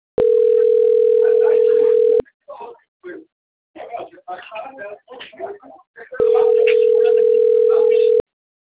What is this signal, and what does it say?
Ringback tone (North America), 440 + 480 Hz, with speech.
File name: rg-1600-unknown- - - .8909